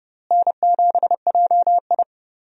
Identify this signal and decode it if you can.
Morse code, heard as a keyed tone, about 31 wpm, 700 Hz. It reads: N7JI
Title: USA Call Signs - 4 Characters - Repeated 3x - 30wpm